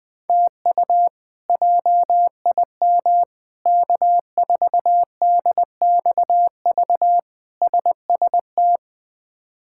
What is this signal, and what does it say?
Morse code, heard as a keyed tone, about 20 wpm, 700 Hz. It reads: TU JIM K4DXV SST
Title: K1USN SST QSO Practice - 20wpm